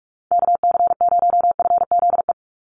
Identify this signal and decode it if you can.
Morse code, heard as a keyed tone, about 44 wpm, 700 Hz. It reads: KC0FZE
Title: USA Call Signs 45wpm